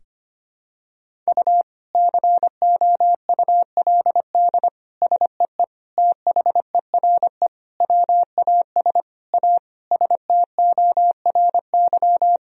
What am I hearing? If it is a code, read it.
Morse code, heard as a keyed tone, about 25 wpm, 700 Hz. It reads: U COULD SEE THERE WAS A STORY
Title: Sentences from Top 500 Words and Abbreviations 25wpm